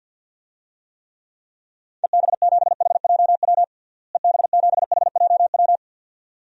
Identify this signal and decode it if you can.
Morse code, heard as a keyed tone, about 53 wpm, 700 Hz. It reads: EB7HJW EB7HJW
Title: DXCC-100 Call Signs - 6 Characters - Repeated 2x - 50wpm